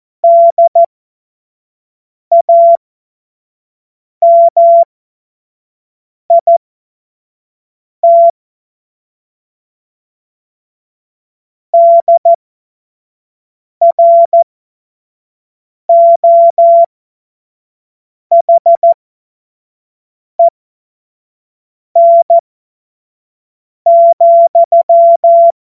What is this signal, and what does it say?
Morse code, heard as a keyed tone, about 14 wpm, spaced as for 5 wpm, 680 Hz. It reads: DAMIT DROHEN,
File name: Morsekurs / Witze / Witz 16. Mai 2023 25 F70